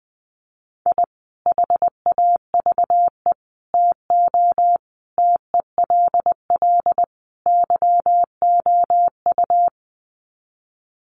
Morse code, heard as a keyed tone, about 20 wpm, 700 Hz. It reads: I HAVE TO TELL YOU
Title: Sets of 5 Words 20wpm